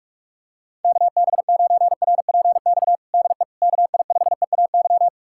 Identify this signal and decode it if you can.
Morse code, heard as a keyed tone, about 46 wpm, 700 Hz. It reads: KB9RPX DE KI5EAY
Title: USA Callsign DE Callsign - 6 Characters - 45wpm